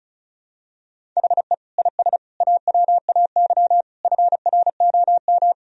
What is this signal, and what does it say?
Morse code, heard as a keyed tone, about 34 wpm, 700 Hz. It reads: HE IS AWAY FROM
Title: Sentences from Top 1000 Words - Review Part 1 - 35wpm